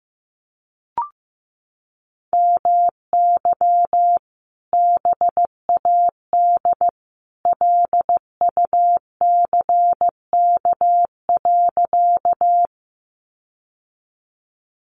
Morse code, heard as a keyed tone, about 15 wpm, 700 Hz. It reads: MY BAD LUCK.
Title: Sentences from Top 1500 Words 15wpm